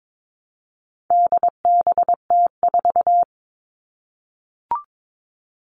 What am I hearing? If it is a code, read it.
Morse code, heard as a keyed tone, about 22 wpm, 700 Hz. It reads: DBT4
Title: All Cut Numbers - 4-Digit 22wpm